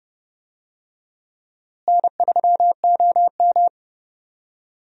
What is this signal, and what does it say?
Morse code, heard as a keyed tone, about 30 wpm, 700 Hz. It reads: N3OM
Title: USA Call Signs - 4 Characters - Repeated 3x - 30wpm